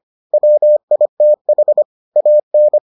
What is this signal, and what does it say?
Morse code, heard as a keyed tone, about 25 wpm, 600 Hz. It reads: WITH AN